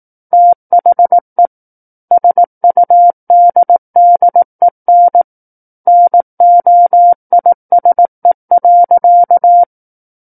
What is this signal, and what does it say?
Morse code, heard as a keyed tone, about 18 wpm, 700 Hz. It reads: THE SUDDEN NOISE.
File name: 18 WPM